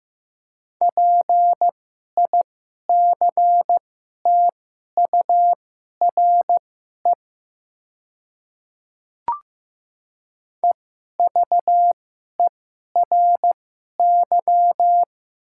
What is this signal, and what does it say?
Morse code, heard as a keyed tone, about 15 wpm, spaced as for 11 wpm, 700 Hz. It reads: PICTURE EVERY
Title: 2x Character Spacing - Top 300 Words - 15wpm